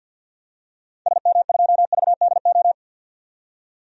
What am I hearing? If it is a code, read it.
Morse code, heard as a keyed tone, about 50 wpm, 700 Hz. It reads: SM2VDO
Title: DXCC-100 Call Signs - 6 Characters - 50wpm